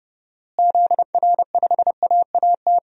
Morse code, heard as a keyed tone, about 32 wpm, 700 Hz. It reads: ZR5AAT
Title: DXCC-100 Call Signs - 6 Characters - Repeated 2x - 30wpm